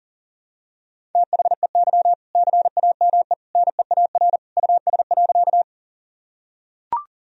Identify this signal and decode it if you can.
Morse code, heard as a keyed tone, about 37 wpm, 700 Hz. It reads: THEY CAME NEAR US.
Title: Sentences from Top 1000 Words - Review Part 1 - 40wpm